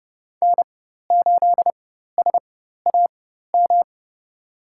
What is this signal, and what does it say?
Morse code, heard as a keyed tone, about 32 wpm, spaced as for 14 wpm, 700 Hz. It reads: N8SAM
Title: USA Call Signs - 5 Characters - 4x Character Spacing - 30wpm